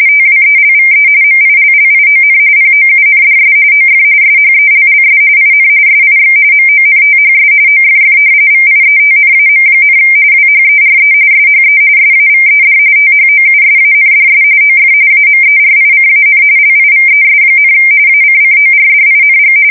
RTTY, about 45.45 baud. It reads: TWELVE MILLION POUNDS OF SHELLS IN SIX MONTHS OFF VIETNAM, WHERE SHE SAVED THOUSANDS OF AMERICAN LIVES, ACCORDING
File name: fsk170 / battleshipnj